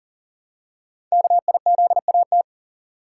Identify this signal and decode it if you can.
Morse code, heard as a keyed tone, about 40 wpm, 700 Hz. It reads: KIZAT
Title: Code Groups - 5 Letters - 40wpm